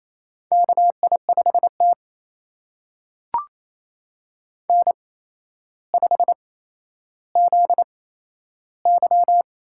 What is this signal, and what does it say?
Morse code, heard as a keyed tone, about 28 wpm, spaced as for 12 wpm, 700 Hz. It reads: KI5T N5ZY
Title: USA Call Signs - 4 Characters - 8x Character Spacing - 28wpm